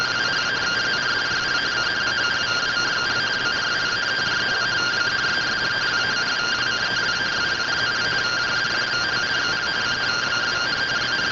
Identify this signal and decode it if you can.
Two-tone teleprinter signal, about 170 Hz shift.